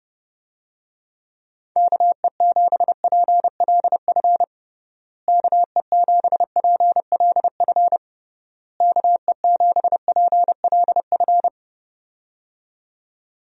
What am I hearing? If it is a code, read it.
Morse code, heard as a keyed tone, about 28 wpm, 700 Hz. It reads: KE7PLF KE7PLF KE7PLF
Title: USA Call Signs - 6 Characters - Repeated 3x - 30wpm